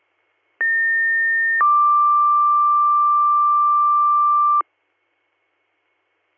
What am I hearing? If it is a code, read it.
Two-tone page, heard as a one-second tone, then a three-second tone, about 1800 then 1180 Hz. It is heard from 0.6 to 4.6 s.